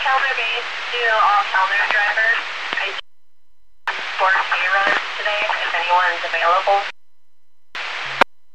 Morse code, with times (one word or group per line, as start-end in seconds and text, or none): none